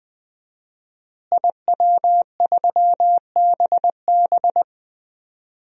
Morse code, heard as a keyed tone, about 21 wpm, 700 Hz. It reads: IW3BB
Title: DXCC-100 Call Signs - 5 Characters - Repeated 3x - 20wpm